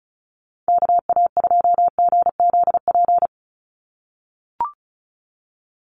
Morse code, heard as a keyed tone, about 34 wpm, 700 Hz. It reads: KA2GZP
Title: USA Call Signs 35wpm